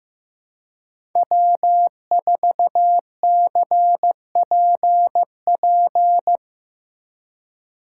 Morse code, heard as a keyed tone, about 17 wpm, 700 Hz. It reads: W4CPP
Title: Speed Racing - USA 1x3 Call Signs 15wpm